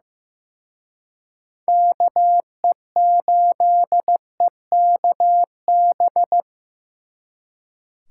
Morse code, heard as a keyed tone, about 18 wpm, 700 Hz. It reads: KE8EKB
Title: Speed Racing - USA 2x3 Call Signs 15wpm